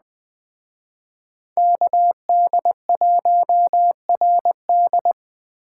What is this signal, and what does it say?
Morse code, heard as a keyed tone, about 20 wpm, 700 Hz. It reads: KD1RD
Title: Speed Racing - USA 2x2 Call Signs 17wpm